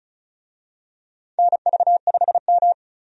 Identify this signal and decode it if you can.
Morse code, heard as a keyed tone, about 36 wpm, 700 Hz. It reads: NV5M
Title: USA Call Signs - 4 Characters - 35wpm